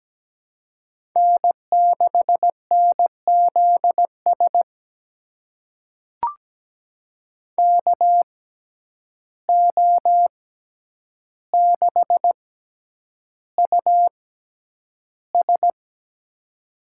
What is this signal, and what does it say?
Morse code, heard as a keyed tone, about 17 wpm, spaced as for 9 wpm, 700 Hz. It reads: N6NZS KO6US
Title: USA Call Signs - 5 Characters - 6x Character Spacing - 17wpm